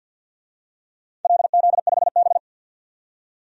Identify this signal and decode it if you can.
Morse code, heard as a keyed tone, about 52 wpm, 700 Hz. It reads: LZ5B
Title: DXCC-100 Call Signs - 4 Characters - 50wpm